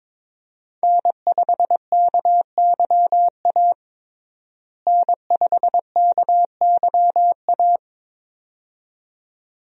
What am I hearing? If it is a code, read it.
Morse code, heard as a keyed tone, about 22 wpm, 700 Hz. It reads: N5KYA N5KYA
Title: USA Call Signs - 5 Characters - Repeated 2x - 22wpm